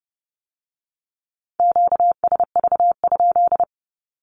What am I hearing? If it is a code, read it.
Morse code, heard as a keyed tone, about 30 wpm, 700 Hz. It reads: QSV?
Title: Q-Codes with Spoken Meaning - 30wpm